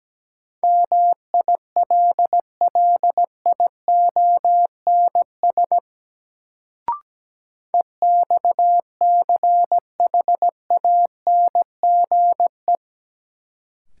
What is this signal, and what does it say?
Morse code, heard as a keyed tone, about 17 wpm, 700 Hz. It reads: MILLIONS EXCHANGE
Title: Sets of 5 Words - New Word Drill - 17wpm